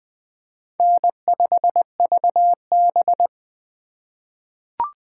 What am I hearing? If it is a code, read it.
Morse code, heard as a keyed tone, about 20 wpm, 700 Hz. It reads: N5VB
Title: USA Call Signs - 4 Characters - 3x Character Spacing - 20wpm